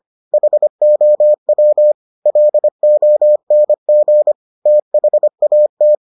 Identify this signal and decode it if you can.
Morse code, heard as a keyed tone, about 25 wpm, 600 Hz. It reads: HOW LONG THAT